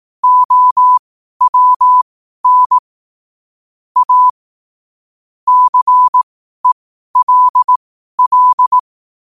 Morse code, heard as a keyed tone, about 18 wpm, spaced as for 12 wpm, 1000 Hz. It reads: OWN A CELL
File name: qotdfiles / 12wpm